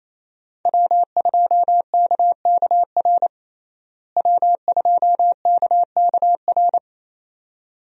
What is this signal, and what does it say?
Morse code, heard as a keyed tone, about 27 wpm, 700 Hz. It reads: W2KKR W2KKR
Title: USA Call Signs - 5 Characters - Repeated 2x - 28wpm